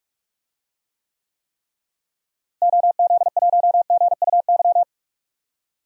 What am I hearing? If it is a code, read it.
Morse code, heard as a keyed tone, about 46 wpm, 700 Hz. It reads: OZ1GUY
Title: DXCC-100 Call Signs - 6 Characters - Repeated 3x - 45wpm